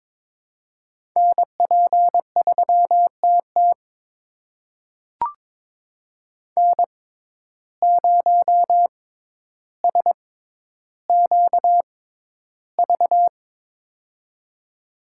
Morse code, heard as a keyed tone, about 23 wpm, spaced as for 11 wpm, 700 Hz. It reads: NP3TT N0SQV
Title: USA Call Signs - 5 Characters - 6x Character Spacing - 22wpm